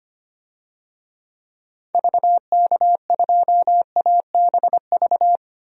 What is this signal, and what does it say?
Morse code, heard as a keyed tone, about 25 wpm, 700 Hz. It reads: VK2ABV
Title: DXCC-100 Call Signs - 6 Characters - 25wpm